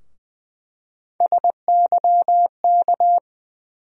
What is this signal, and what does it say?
Morse code, heard as a keyed tone, about 20 wpm, 700 Hz. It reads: SYK
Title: USA Call Sign Postfixes - 3 Letters 20wpm